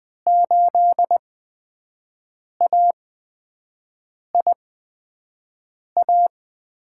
Morse code, heard as a keyed tone, about 20 wpm, spaced as for 5 wpm, 700 Hz. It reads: 8AIA
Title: USA Call Signs - 6 Characters - 8x Character Spacing - 20wpm